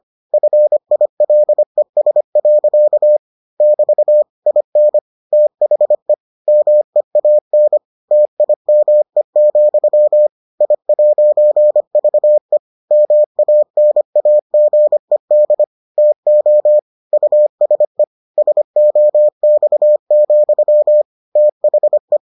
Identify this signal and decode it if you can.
Morse code, heard as a keyed tone, about 25 wpm, 600 Hz. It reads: FILES. <BT> IN THE MEAN TIME, I'VE MANAGED TO USE SOX, THE